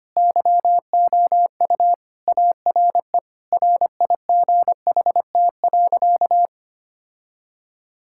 Morse code, heard as a keyed tone, about 25 wpm, 700 Hz. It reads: YOU ARE RIGHT.
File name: practice / 25wpm / Sentences with 3-Word Phrases - 25wpm